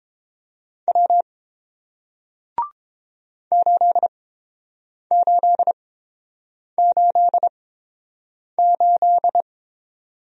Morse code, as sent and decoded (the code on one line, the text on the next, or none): .-- / ---.. ---.. ---.. ---..
W 8888